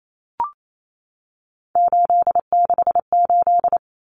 Morse code, heard as a keyed tone, about 28 wpm, 700 Hz. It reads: 868
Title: Numbers - 3-Digit 28wpm